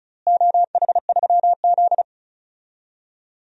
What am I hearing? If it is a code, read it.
Morse code, heard as a keyed tone, about 34 wpm, 700 Hz. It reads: OH3Z
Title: DXCC-100 Call Signs - 4 Characters - Repeated 2x - 35wpm